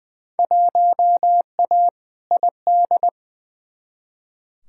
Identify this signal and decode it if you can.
Morse code, heard as a keyed tone, about 20 wpm, 700 Hz. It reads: 1A ID
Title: Field Day Exchanges - 20wpm